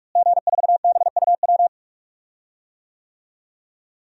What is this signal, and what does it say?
Morse code, heard as a keyed tone, about 42 wpm, 700 Hz. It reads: G4BUW
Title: DXCC-100 Call Signs - 5 Characters - Repeated 3x - 45wpm